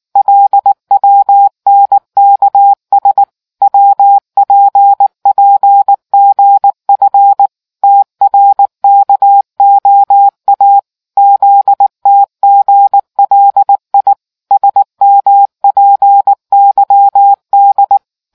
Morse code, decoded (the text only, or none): LWNKS WPPGF TRKOA ZTGLI SMPYD